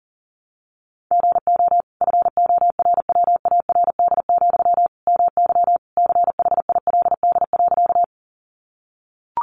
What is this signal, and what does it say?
Morse code, heard as a keyed tone, about 37 wpm, 700 Hz. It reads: GO FORWARD, MY CHILD.
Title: Sentences from Top 1300 Words 40wpm